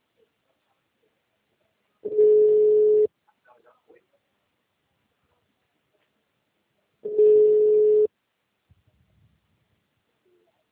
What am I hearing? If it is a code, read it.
Ringback tone (Europe, 425 Hz).
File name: out- -2015